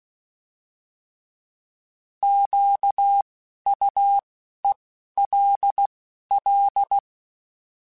Morse code, heard as a keyed tone, about 16 wpm, 800 Hz. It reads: Q U E L L
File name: Alphabetical / Alphabetical mp3s / Qs @10